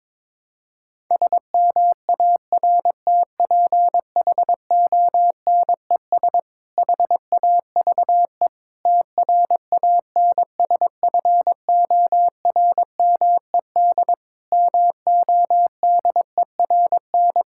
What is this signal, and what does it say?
Morse code, heard as a keyed tone, about 22 wpm, 700 Hz. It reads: SMARTPHONES HAVE TRANSFORMED MODERN
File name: Difficult Sentences with Hints - 22wpm